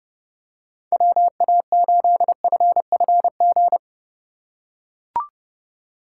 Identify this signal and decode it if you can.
Morse code, heard as a keyed tone, about 27 wpm, 700 Hz. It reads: WA8FFG